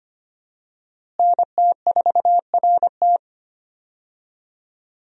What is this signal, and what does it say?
Morse code, heard as a keyed tone, about 25 wpm, 700 Hz. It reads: NT4RT
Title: USA Call Signs - 5 Characters - Repeated 3x - 25wpm